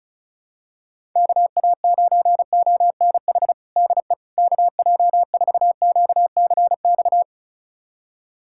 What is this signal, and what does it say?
Morse code, heard as a keyed tone, about 36 wpm, 700 Hz. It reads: KA9ONH DE KJ4QCX
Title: USA Callsign DE Callsign - 6 Characters - 35wpm